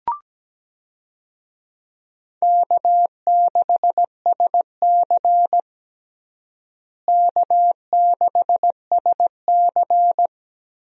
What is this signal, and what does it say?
Morse code, heard as a keyed tone, about 17 wpm, 700 Hz. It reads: K6SC K6SC
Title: USA Call Signs - 4 Characters - Repeated 2x - 17wpm